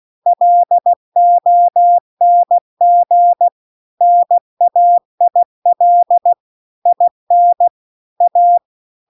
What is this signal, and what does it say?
Morse code, heard as a keyed tone, about 16 wpm, 700 Hz. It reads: LONG NAIL IN A